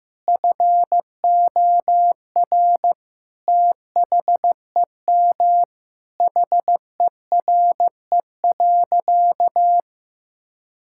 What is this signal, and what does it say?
Morse code, heard as a keyed tone, about 15 wpm, 700 Hz. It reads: FOR THEM HERE.
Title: Sentences from Top 1000 Words - Review Part 5 - 15wpm